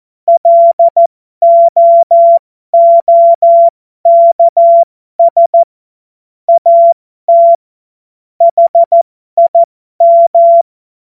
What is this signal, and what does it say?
Morse code, heard as a keyed tone, about 14 wpm, spaced as for 12 wpm, 680 Hz. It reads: LOOKS AT HIM
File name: Morsekurs / Witze